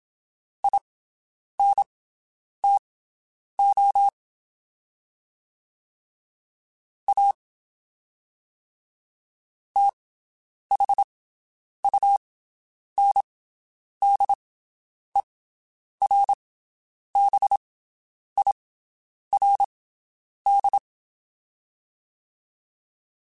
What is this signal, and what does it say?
Morse code, heard as a keyed tone, about 27 wpm, spaced as for 9 wpm, 800 Hz. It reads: INTO A THUNDERBIRD
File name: (spaced) / QSOs Spaced-mp3s / WD0BGP 1 @25 sp